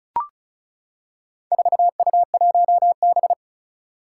Morse code, heard as a keyed tone, about 36 wpm, 700 Hz. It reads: 4U1B
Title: Four Challenging Letter-Number Combinations 35wpm